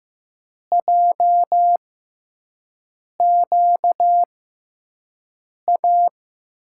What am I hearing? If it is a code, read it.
Morse code, heard as a keyed tone, about 15 wpm, spaced as for 5 wpm, 700 Hz. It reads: JQA
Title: USA Call Signs - 5 Characters - 6x Character Spacing - 15wpm